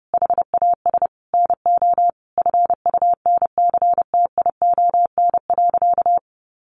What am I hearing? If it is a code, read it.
Morse code, heard as a keyed tone, about 28 wpm, 700 Hz. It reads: HAS NO FUNCTION.
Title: Sentences from Top 1100 Words 30wpm